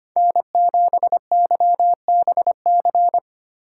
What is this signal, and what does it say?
Morse code, heard as a keyed tone, about 25 wpm, 700 Hz. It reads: N7YBC
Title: USA Call Signs - 5 Characters - 4x Character Spacing - 25wpm